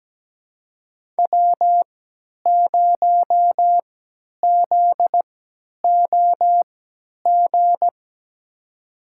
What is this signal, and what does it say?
Morse code, heard as a keyed tone, about 17 wpm, spaced as for 10 wpm, 700 Hz. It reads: W0ZOG
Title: USA Call Signs - 5 Characters - 3x Character Spacing - 17wpm